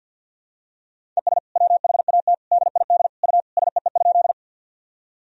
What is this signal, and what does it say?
Morse code, heard as a keyed tone, about 50 wpm, 700 Hz. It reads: ES WHAT DID U SEE?
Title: Sentences from Top 100 Words and Abbreviations 50wpm